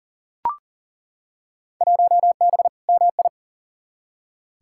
Morse code, heard as a keyed tone, about 40 wpm, 700 Hz. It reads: 1B MI
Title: Field Day Exchanges - 40wpm